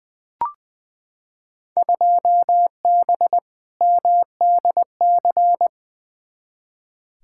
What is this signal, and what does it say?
Morse code, heard as a keyed tone, about 20 wpm, 700 Hz. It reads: 2B MDC
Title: Field Day Exchanges - 20wpm